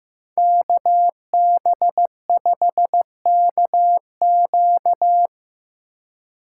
Morse code, heard as a keyed tone, about 18 wpm, 700 Hz. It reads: KB5KQ
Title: Speed Racing - USA 2x2 Call Signs 15wpm